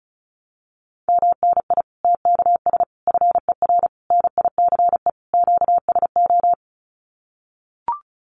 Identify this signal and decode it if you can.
Morse code, heard as a keyed tone, about 34 wpm, 700 Hz. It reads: MNI TKS FER NICE QSO
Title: Common QSO Phrases 35wpm